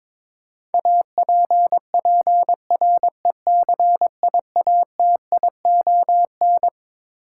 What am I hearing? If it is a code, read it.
Morse code, heard as a keyed tone, about 22 wpm, 700 Hz. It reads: APPRECIATION
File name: Twelve Letter Words - 22wpm